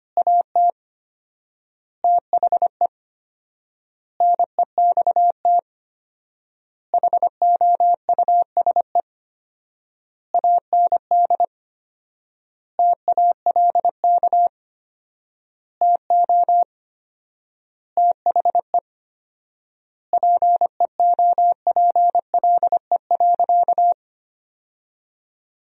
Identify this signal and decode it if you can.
Morse code, heard as a keyed tone, about 25 wpm, 700 Hz. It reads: AT THE NEXT HOUSE AND TALK TO THE PEOPLE.
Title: 4x Word-Spacing - Sentences from Top 400 Words - 25wpm